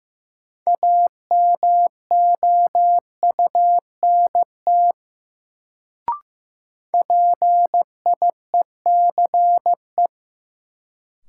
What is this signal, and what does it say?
Morse code, heard as a keyed tone, about 15 wpm, 700 Hz. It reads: AMOUNT PIECE
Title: Sets of 3 Words - New Word Drill - 15wpm